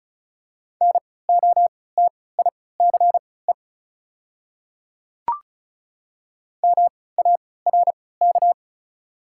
Morse code, heard as a keyed tone, about 36 wpm, spaced as for 20 wpm, 700 Hz. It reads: NOTICE MARK
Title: 3x Character Spacing - Top 500 Words - 35wpm